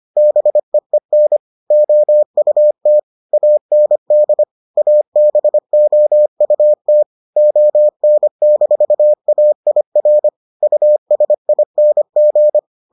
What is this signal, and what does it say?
Morse code, heard as a keyed tone, about 25 wpm, 600 Hz. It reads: BEEN OUT AND ABOUT ON-AIR USING